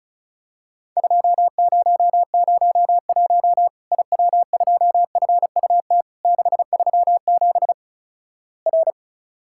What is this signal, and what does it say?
Morse code, heard as a keyed tone, about 36 wpm, 700 Hz. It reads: 2001 IW2FUT 637 R
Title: Worked All Europe DX Contest - QTC Exchange - 35wpm